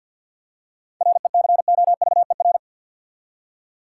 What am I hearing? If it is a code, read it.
Morse code, heard as a keyed tone, about 50 wpm, 700 Hz. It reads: RECOVER
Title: Seven Letter Words - 50wpm